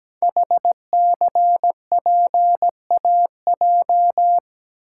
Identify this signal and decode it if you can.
Morse code, heard as a keyed tone, about 17 wpm, 700 Hz. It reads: HCPAJ
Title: Code Groups - 5 Letters - 17wpm